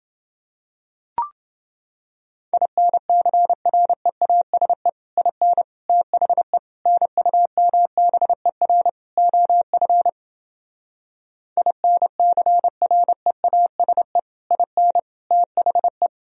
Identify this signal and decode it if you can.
Morse code, heard as a keyed tone, about 29 wpm, 700 Hz. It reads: INCREASE IN THE NUMBER OF INCREASE IN THE